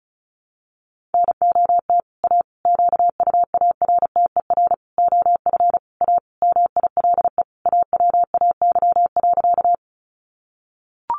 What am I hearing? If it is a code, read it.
Morse code, heard as a keyed tone, about 34 wpm, 700 Hz. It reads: NOT A QUARTER OF A MILE AWAY.
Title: Sentences from Top 1300 Words 35wpm